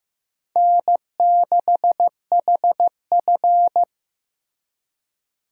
Morse code, heard as a keyed tone, about 15 wpm, 700 Hz. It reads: N6HF
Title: USA Call Signs - 4 Characters - Repeated 2x - 15wpm